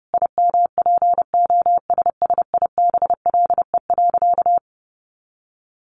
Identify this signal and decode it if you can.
Morse code, heard as a keyed tone, about 30 wpm, 700 Hz. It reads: IMPOSSIBLE.
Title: Sentences from Top 1100 Words 30wpm